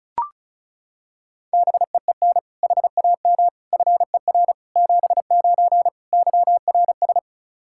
Morse code, heard as a keyed tone, about 34 wpm, 700 Hz. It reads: BEEN HAM FER 79 YRS
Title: Practice QSO - Exchange 3 - More Info - 35wpm